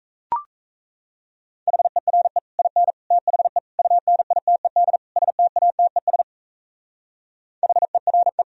Morse code, heard as a keyed tone, about 33 wpm, 700 Hz. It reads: HERE IN THE UNITED STATES HERE
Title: Speed-Racing - Sets of 5 Words 28wpm